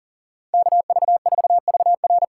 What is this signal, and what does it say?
Morse code, heard as a keyed tone, about 40 wpm, 700 Hz. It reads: KV4VR
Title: USA Call Signs - 4x Character Spacing - 40wpm